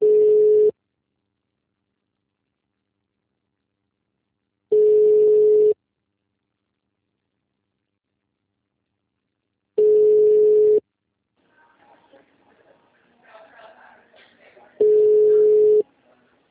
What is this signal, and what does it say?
Ringback tone (Europe, 425 Hz), with speech.
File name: out- -2005- - - .7429